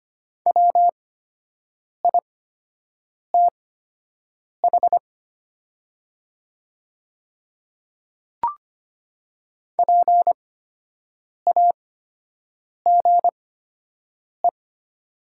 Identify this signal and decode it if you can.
Morse code, heard as a keyed tone, about 25 wpm, spaced as for 7 wpm, 700 Hz. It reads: WITH PAGE
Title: 8x Character Spacing - Top 500 Words - 25wpm